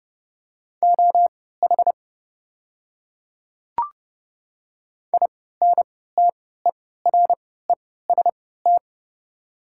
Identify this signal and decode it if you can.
Morse code, heard as a keyed tone, about 32 wpm, spaced as for 17 wpm, 690 Hz. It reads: OH INTEREST
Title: 3x Character Spacing - Top 500 Words - 30wpm